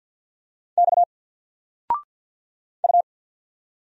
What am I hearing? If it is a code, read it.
Morse code, heard as a keyed tone, about 49 wpm, 700 Hz. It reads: X U
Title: Single Letter-Number Alt 50wpm